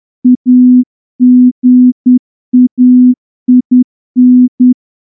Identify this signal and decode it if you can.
Morse code, heard as a keyed tone, about 10 wpm, 250 Hz. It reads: AGAIN
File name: morse-training-10WPM